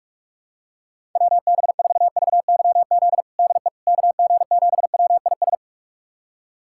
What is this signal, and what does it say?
Morse code, heard as a keyed tone, about 46 wpm, 700 Hz. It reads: WB4VYZ DE KG7WIS
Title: USA Callsign DE Callsign 45wpm